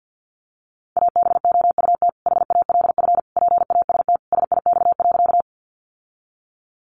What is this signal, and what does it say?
Morse code, heard as a keyed tone, about 50 wpm, 700 Hz. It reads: ABOUT HALF PAST SIX.
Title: Sentences from Top 600 Words 50wpm